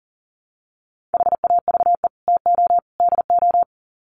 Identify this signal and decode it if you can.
Morse code, heard as a keyed tone, about 40 wpm, 700 Hz. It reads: HAVE TO DO
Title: Sets of 3 Words 40wpm